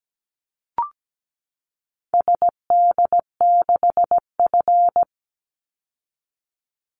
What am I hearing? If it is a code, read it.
Morse code, heard as a keyed tone, about 17 wpm, 700 Hz. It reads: SD6F
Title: Call Signs Repeated 3x - 17wpm